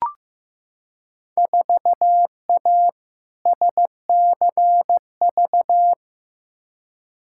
Morse code, heard as a keyed tone, about 15 wpm, 700 Hz. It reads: 4A SCV
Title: Field Day Exchanges - 15wpm